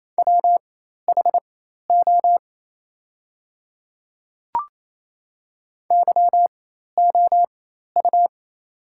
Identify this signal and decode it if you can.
Morse code, heard as a keyed tone, about 28 wpm, spaced as for 13 wpm, 690 Hz. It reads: WHO YOU
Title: 4x Character Spacing - Top 100 Words - 28wpm